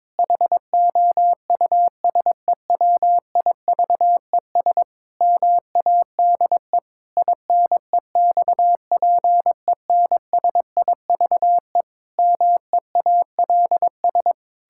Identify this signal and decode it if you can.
Morse code, heard as a keyed tone, about 22 wpm, 700 Hz. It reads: HOUSEWIVES MADE INEXPENSIVE MEALS
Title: Pangram Challenge v2 - 22wpm